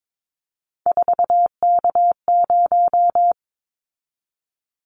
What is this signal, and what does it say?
Morse code, heard as a keyed tone, about 22 wpm, 700 Hz. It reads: 4K0
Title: Callsign Prefixes for Top 100 DXCC - 22wpm